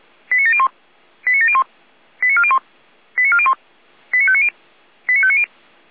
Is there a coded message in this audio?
Yes: five-tone selective call (ZVEI) from 0.3 to 5.4 s, heard as quick runs of five tones, tones of about 70 ms. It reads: 89701 89701 89401 89401 89590 89590